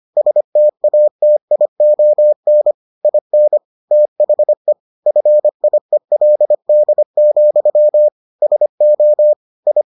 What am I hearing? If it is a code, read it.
Morse code, heard as a keyed tone, about 25 wpm, 600 Hz. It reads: STATION IN THE FIELD, SO I